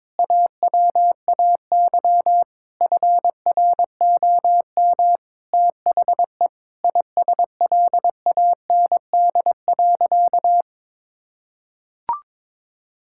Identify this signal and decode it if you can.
Morse code, heard as a keyed tone, about 22 wpm, 700 Hz. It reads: AWAY FROM THE ISLAND.
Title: Sentences from Top 1000 Words - Review Part 5 - 22wpm